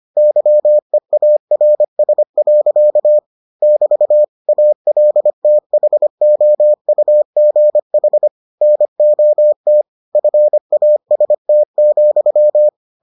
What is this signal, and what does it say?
Morse code, heard as a keyed tone, about 25 wpm, 600 Hz. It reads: YEARS. <BT> ALTHOUGH NOT FAST,